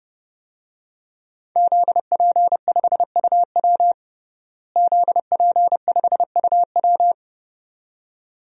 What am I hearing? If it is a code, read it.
Morse code, heard as a keyed tone, about 32 wpm, 700 Hz. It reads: ZP5UW ZP5UW
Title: DXCC-100 Call Signs - 5 Characters - Repeated 2x - 30wpm